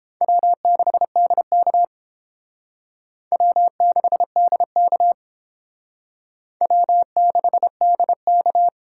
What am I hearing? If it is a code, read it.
Morse code, heard as a keyed tone, about 30 wpm, 700 Hz. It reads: W6DK W6DK W6DK